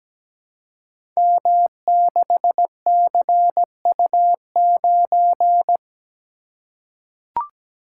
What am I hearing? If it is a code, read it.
Morse code, heard as a keyed tone, about 17 wpm, 700 Hz. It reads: M6CU9
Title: Code Groups - 5 Letter:Numbers - 17wpm